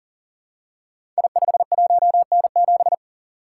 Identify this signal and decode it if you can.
Morse code, heard as a keyed tone, about 42 wpm, 700 Hz. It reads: I51N7
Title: Code Groups - 5 Letter:Numbers - 40wpm